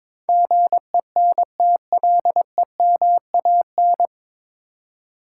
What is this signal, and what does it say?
Morse code, heard as a keyed tone, about 22 wpm, 700 Hz. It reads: GENTLEMAN
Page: Morse Code Ninja practice 22wpm